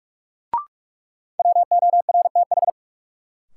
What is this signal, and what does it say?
Morse code, heard as a keyed tone, about 46 wpm, 700 Hz. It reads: WORTH
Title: Sentences from Top 800 Words - New Word Drill - 45wpm